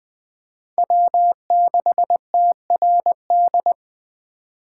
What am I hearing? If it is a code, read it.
Morse code, heard as a keyed tone, about 21 wpm, 700 Hz. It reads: W6TRD
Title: USA Call Signs - 5 Characters - 2x Character Spacing - 20wpm